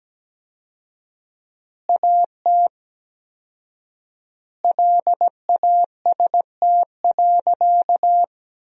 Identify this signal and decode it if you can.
Morse code, heard as a keyed tone, about 17 wpm, 700 Hz. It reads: AT LAST.
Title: 4x Word-Spacing - Sentences from Top 300 Words - 17wpm